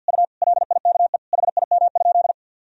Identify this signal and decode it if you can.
Morse code, heard as a keyed tone, about 52 wpm, 700 Hz. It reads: U LIKE HIM?
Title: Sentences from Top 100 Words and Abbreviations 50wpm